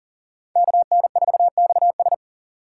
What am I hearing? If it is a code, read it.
Morse code, heard as a keyed tone, about 38 wpm, 700 Hz. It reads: KN4XS
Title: USA Call Signs - 5 Characters - 3x Character Spacing - 40wpm